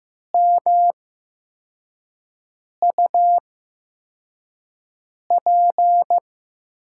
Morse code, heard as a keyed tone, about 15 wpm, spaced as for 4 wpm, 700 Hz. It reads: MUP
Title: USA Call Signs - 5 Characters - 8x Character Spacing - 15wpm